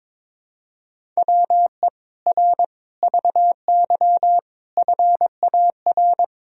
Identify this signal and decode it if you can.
Morse code, heard as a keyed tone, about 22 wpm, 700 Hz. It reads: WE R VY FAR
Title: Sentences from Top 300 Words and Abbreviations 22wpm